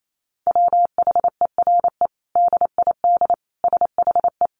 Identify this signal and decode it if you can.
Morse code, heard as a keyed tone, about 28 wpm, 700 Hz. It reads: WHERE DID SHE
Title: Sentences from Top 200 Words 28wpm